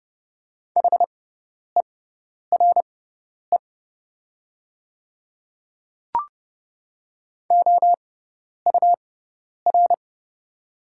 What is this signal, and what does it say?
Morse code, heard as a keyed tone, about 30 wpm, spaced as for 10 wpm, 700 Hz. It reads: HERE OUR
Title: 6x Character Spacing - Top 200 Words - 30wpm